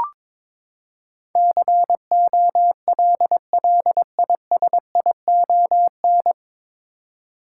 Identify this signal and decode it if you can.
Morse code, heard as a keyed tone, about 22 wpm, 700 Hz. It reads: COLLISION
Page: Morse Code Ninja practice 22wpm